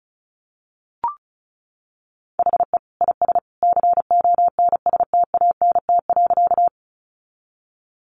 Morse code, heard as a keyed tone, about 34 wpm, 700 Hz. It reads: HE IS CONSTANT.
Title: Sentences from Top 1500 Words 35wpm